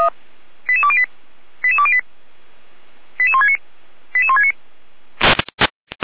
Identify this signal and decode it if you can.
Five-tone selective call (ZVEI), heard as quick runs of five tones, tones of about 70 ms. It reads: 88208 88208 88169 88169